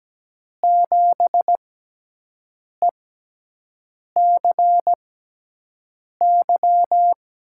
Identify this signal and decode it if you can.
Morse code, heard as a keyed tone, about 17 wpm, spaced as for 6 wpm, 700 Hz. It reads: 7ECY
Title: USA Call Signs - 6 Characters - 6x Character Spacing - 17wpm